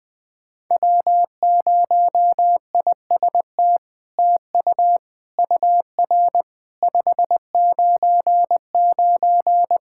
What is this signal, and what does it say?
Morse code, heard as a keyed tone, about 21 wpm, 700 Hz. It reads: W0IST TU UR 599
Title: SOTA QSO Practice 20wpm